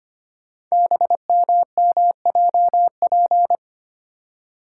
Morse code, heard as a keyed tone, about 25 wpm, 700 Hz. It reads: BMMJP
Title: Code Groups - 5 Letter:Numbers - 25wpm